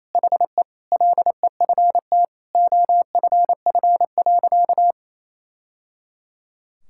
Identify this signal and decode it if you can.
Morse code, heard as a keyed tone, about 27 wpm, 700 Hz. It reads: HE LEFT OFF.
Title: Sentences from Top 1000 Words - Review Part 6 - 28wpm